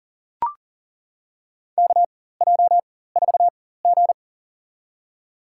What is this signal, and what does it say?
Morse code, heard as a keyed tone, about 39 wpm, spaced as for 18 wpm, 700 Hz. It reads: KJ4G
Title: USA Call Signs - 4 Characters - 4x Character Spacing - 40wpm